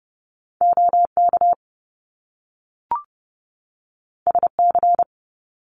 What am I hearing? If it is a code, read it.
Morse code, heard as a keyed tone, about 30 wpm, 700 Hz. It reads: OK SC
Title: US States 30wpm